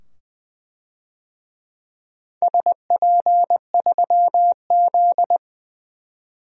Morse code, heard as a keyed tone, about 20 wpm, 700 Hz. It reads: SP3Z
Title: DXCC-100 Call Signs - 4 Characters - 20wpm